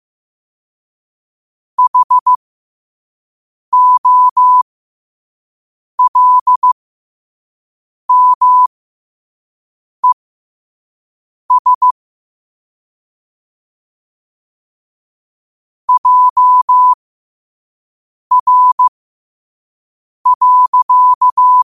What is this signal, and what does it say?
Morse code, heard as a keyed tone, about 15 wpm, spaced as for 5 wpm, 1000 Hz. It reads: HOLMES JR.
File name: qotdfiles / 5wpm